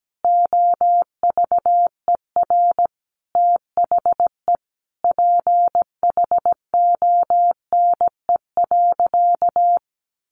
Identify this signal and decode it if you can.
Morse code, heard as a keyed tone, about 17 wpm, 700 Hz. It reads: OVER THE PHONE.
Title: Sentences from Top 1100 Words 17wpm